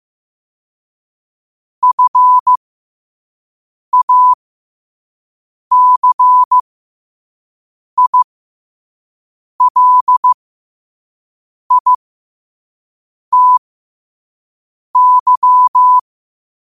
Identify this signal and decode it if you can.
Morse code, heard as a keyed tone, about 15 wpm, spaced as for 5 wpm, 1000 Hz. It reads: FACILITY
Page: qotd-at-5-wpm